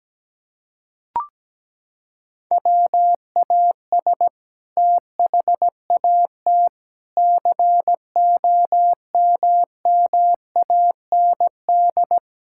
Morse code, heard as a keyed tone, about 17 wpm, 700 Hz. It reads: WAS THAT COMMAND
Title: Sentences from Top 1600 Words 17wpm